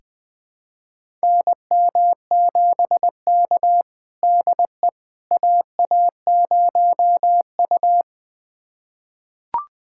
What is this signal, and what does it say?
Morse code, heard as a keyed tone, about 20 wpm, 700 Hz. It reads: NM7K DE AA0U
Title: USA Callsign DE Callsign - 4 Characters - 20wpm